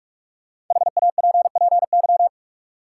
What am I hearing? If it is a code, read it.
Morse code, heard as a keyed tone, about 42 wpm, 700 Hz. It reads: HAPPY
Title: Sentences from Top 1000 Words - New Word Drill - 45wpm